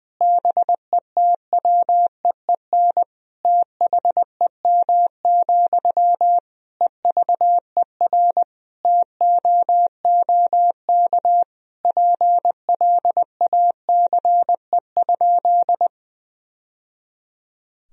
Morse code, heard as a keyed tone, about 20 wpm, 700 Hz. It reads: BETWEEN THEM, EVER TOOK PLACE?
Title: Sentences from Top 1000 Words - Review Part 1 - 20wpm